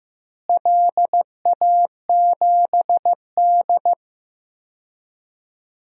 Morse code, heard as a keyed tone, about 15 wpm, 700 Hz. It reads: LA7D
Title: DXCC-100 Call Signs - 4 Characters - Repeated 2x - 15wpm